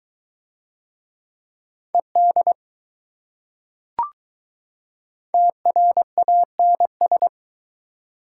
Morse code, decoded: ED TRANS

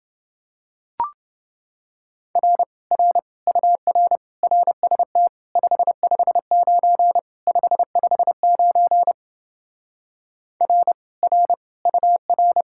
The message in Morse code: R R UR RST 559 559 R R UR

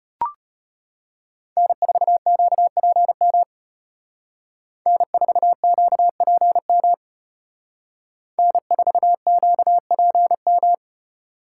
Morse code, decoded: N4QPM N4QPM N4QPM

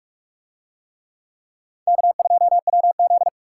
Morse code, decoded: K2WZ

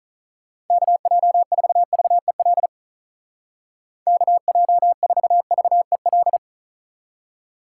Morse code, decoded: KJ4VEL KJ4VEL

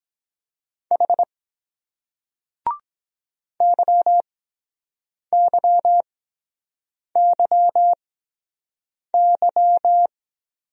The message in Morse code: H YYYY

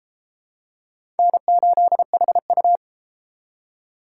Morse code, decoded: N8HU

